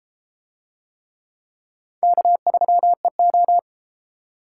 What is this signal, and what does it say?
Morse code, heard as a keyed tone, about 30 wpm, 700 Hz. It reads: K3EO